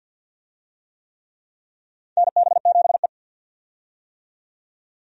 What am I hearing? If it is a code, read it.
Morse code, heard as a keyed tone, about 48 wpm, 700 Hz. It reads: NB7E
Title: USA Call Signs - 4 Characters - 50wpm